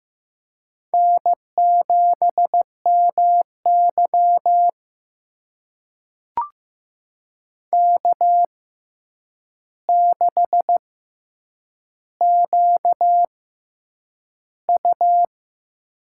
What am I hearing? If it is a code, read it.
Morse code, heard as a keyed tone, about 15 wpm, spaced as for 8 wpm, 700 Hz. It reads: N7MY K6QU
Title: USA Call Signs - 4 Characters - 6x Character Spacing - 15wpm